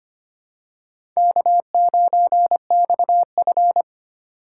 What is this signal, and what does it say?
Morse code, heard as a keyed tone, about 25 wpm, 700 Hz. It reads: K9XF
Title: USA Call Signs - 4 Characters - 2x Character Spacing - 25wpm